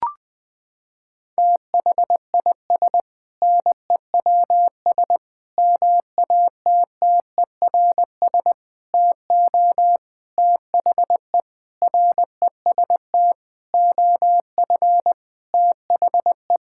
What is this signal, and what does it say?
Morse code, heard as a keyed tone, about 20 wpm, 700 Hz. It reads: THIS NEWS MATTERS TO THE REST OF THE
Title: Sentences with 6-Word Phrases - 20wpm